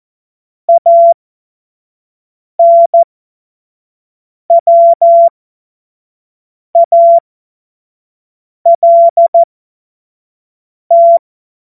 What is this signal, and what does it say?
Morse code, heard as a keyed tone, about 14 wpm, spaced as for 5 wpm, 680 Hz. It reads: ANWALT